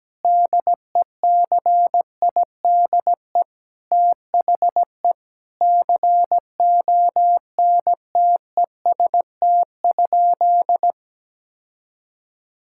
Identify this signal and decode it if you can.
Morse code, heard as a keyed tone, about 17 wpm, 700 Hz. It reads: DECIDE THE CONTEST?
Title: Sentences from Top 1600 Words 17wpm